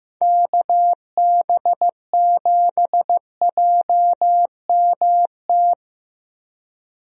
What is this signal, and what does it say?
Morse code, heard as a keyed tone, about 15 wpm, 700 Hz. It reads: KB7JMT
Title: USA Call Signs - 6 Characters - Repeated 3x - 15wpm